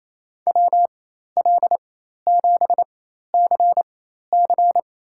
Morse code, heard as a keyed tone, about 28 wpm, spaced as for 13 wpm, 700 Hz. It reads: WL7CC
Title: USA Call Signs - 5 Characters - 4x Character Spacing - 28wpm